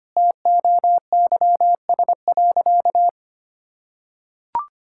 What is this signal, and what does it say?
Morse code, heard as a keyed tone, about 25 wpm, 700 Hz. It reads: TOYS.
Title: Pangram Challenge v2 - 25wpm